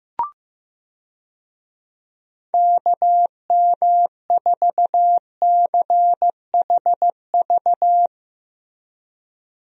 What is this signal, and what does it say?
Morse code, heard as a keyed tone, about 15 wpm, 700 Hz. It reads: KM4CHV
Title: USA Call Signs - 6 Characters - Repeated 2x - 15wpm